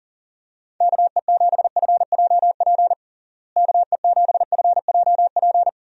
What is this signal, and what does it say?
Morse code, heard as a keyed tone, about 40 wpm, 700 Hz. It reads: KE7FJP KE7FJP
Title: USA Call Signs - 6 Characters - Repeated 3x - 40wpm